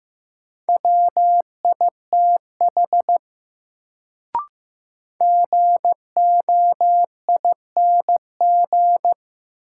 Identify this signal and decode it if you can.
Morse code, heard as a keyed tone, about 15 wpm, 700 Hz. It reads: WITH GOING
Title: Sets of 2 Words - New Word Drill - 15wpm